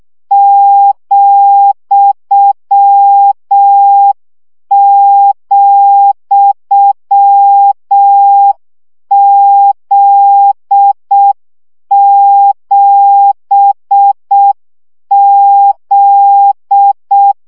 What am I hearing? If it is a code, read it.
Morse code, heard as a keyed tone, about 6 wpm, 800 Hz. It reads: ,,Z7Z
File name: CW gyakorlas z 7 vesszo 5 baud